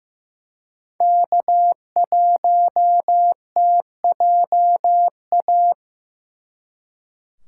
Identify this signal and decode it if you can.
Morse code, heard as a keyed tone, about 18 wpm, 700 Hz. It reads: K1TJA
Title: Speed Racing - USA 1x3 Call Signs 15wpm